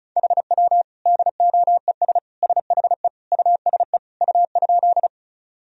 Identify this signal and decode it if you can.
Morse code, heard as a keyed tone, about 34 wpm, 700 Hz. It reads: HW DOES SHE USE U?
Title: Sentences from Top 200 Words and Abbreviations 35wpm